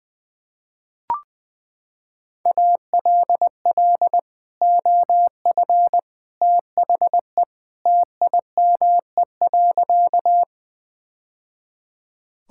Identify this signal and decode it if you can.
Morse code, heard as a keyed tone, about 21 wpm, 700 Hz. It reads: ALL OF THE TIME.
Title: Sentences with 3-Word Phrases - 20wpm